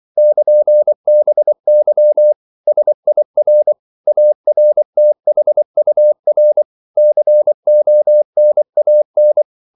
Morse code, heard as a keyed tone, about 24 wpm, 600 Hz. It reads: <KN>BY SIR ARTHUR CONAN
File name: CW / Practice / EnglishQuotations0224